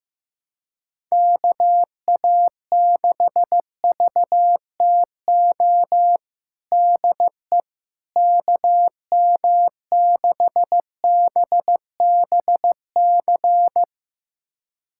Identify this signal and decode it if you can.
Morse code, heard as a keyed tone, about 15 wpm, 700 Hz. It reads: KA6VTO DE KM6BBC
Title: USA Callsign DE Callsign 15wpm